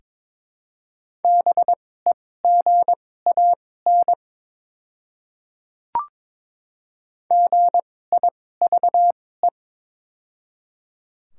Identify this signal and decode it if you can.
Morse code, heard as a keyed tone, about 22 wpm, spaced as for 16 wpm, 700 Hz. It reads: BEGAN GIVE
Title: 2x Character Spacing - Top 300 Words - 22wpm